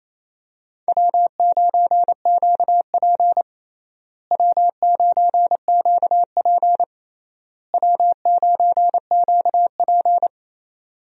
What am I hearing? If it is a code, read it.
Morse code, heard as a keyed tone, about 28 wpm, 700 Hz. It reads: W9QP W9QP W9QP
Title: USA Call Signs - 4 Characters - Repeated 3x - 28wpm